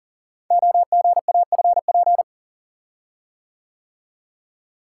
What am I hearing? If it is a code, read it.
Morse code, heard as a keyed tone, about 40 wpm, 690 Hz. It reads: OGAFP
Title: Code Groups - 5 Letters - 40wpm